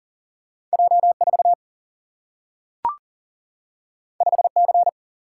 Morse code, heard as a keyed tone, about 40 wpm, 700 Hz. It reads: J4 5C
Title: Single-Letter Single-Number 40wpm